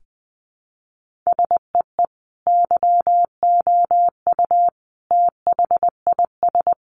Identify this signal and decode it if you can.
Morse code, heard as a keyed tone, about 20 wpm, 700 Hz. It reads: SEE YOU THIS
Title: Sentences from Top 1500 Words 20wpm